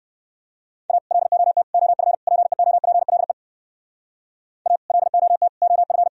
Morse code, heard as a keyed tone, about 58 wpm, 700 Hz. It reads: A LOT OF PEOPLE A LOT OF